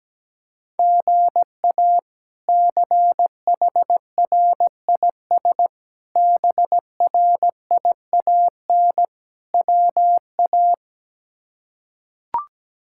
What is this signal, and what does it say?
Morse code, heard as a keyed tone, about 17 wpm, 700 Hz. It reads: GA CHRIS BRIAN WA
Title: K1USN SST QSO Practice - 17wpm